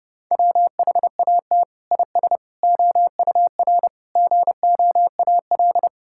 Morse code, heard as a keyed tone, about 32 wpm, 700 Hz. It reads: WHAT IS OUR GOAL
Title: Sentences from Top 1000 Words - Review Part 5 - 30wpm